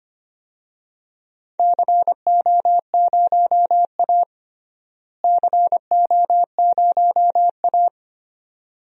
Morse code, heard as a keyed tone, about 25 wpm, 700 Hz. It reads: CO0A CO0A
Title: DXCC-100 Call Signs - 4 Characters - Repeated 2x - 25wpm